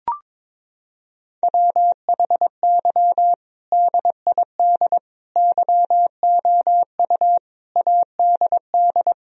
Morse code, heard as a keyed tone, about 22 wpm, 700 Hz. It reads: WHY DID YOU ADD
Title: Sentences from Top 1000 Words - Review Part 1 - 22wpm